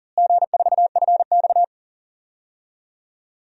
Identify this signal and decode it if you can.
Morse code, heard as a keyed tone, about 40 wpm, 700 Hz. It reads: G4FX
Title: DXCC-100 Call Signs - 4 Characters - Repeated 2x - 40wpm